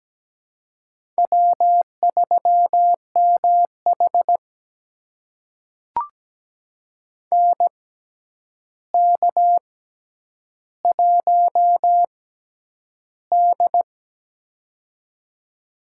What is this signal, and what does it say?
Morse code, heard as a keyed tone, about 17 wpm, spaced as for 9 wpm, 700 Hz. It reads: W3MH NK1D
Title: USA Call Signs - 4 Characters - 6x Character Spacing - 17wpm